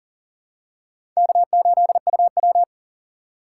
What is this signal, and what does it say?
Morse code, heard as a keyed tone, about 38 wpm, 700 Hz. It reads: K8UW
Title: USA Call Signs - 4 Characters - 3x Character Spacing - 40wpm